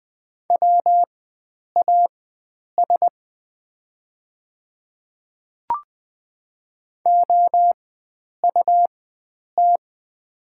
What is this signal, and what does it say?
Morse code, heard as a keyed tone, about 20 wpm, spaced as for 9 wpm, 690 Hz. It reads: WAS OUT
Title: 4x Character Spacing - Top 100 Words - 20wpm